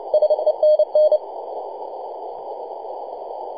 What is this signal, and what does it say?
Morse code, heard as a keyed tone, about 29 wpm, 610 Hz. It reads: SENN
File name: posts / 2015-nov-30-0843z-ce0y-r4waa-7-028mhz-cw